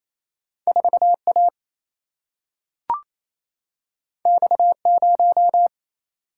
Morse code, heard as a keyed tone, about 28 wpm, 700 Hz. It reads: 4A X0
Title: Single-Letter Single-Number 28wpm